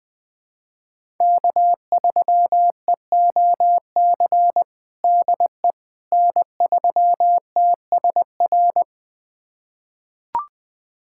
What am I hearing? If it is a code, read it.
Morse code, heard as a keyed tone, about 20 wpm, 700 Hz. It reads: K3EOC DE N3TSR
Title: USA Callsign DE Callsign - 5 Characters - 20wpm